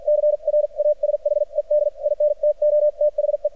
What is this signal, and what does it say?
Morse code, heard as a keyed tone, about 53 wpm, spaced as for 43 wpm, 600 Hz. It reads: CRASHED INTO THE
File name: archived / CRSnet